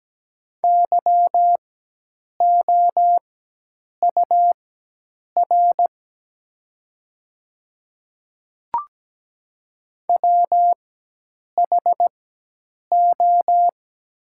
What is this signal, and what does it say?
Morse code, heard as a keyed tone, about 17 wpm, spaced as for 8 wpm, 700 Hz. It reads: YOUR WHO
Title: 4x Character Spacing - Top 100 Words - 17wpm